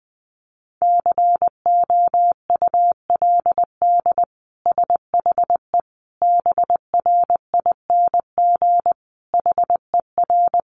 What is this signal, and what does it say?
Morse code, heard as a keyed tone, about 19 wpm, 700 Hz. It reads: COULD SHE BRING HER
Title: Sentences from Top 1400 Words 20wpm